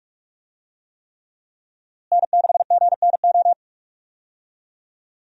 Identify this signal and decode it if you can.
Morse code, heard as a keyed tone, about 46 wpm, 700 Hz. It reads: N6GNO
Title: USA Call Signs - 5 Characters - Repeated 3x - 45wpm